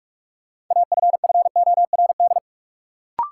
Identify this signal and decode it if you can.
Morse code, heard as a keyed tone, about 44 wpm, 700 Hz. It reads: AFFORD
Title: Sentences from Top 1100 Words - New Word Drill Alt - 45wpm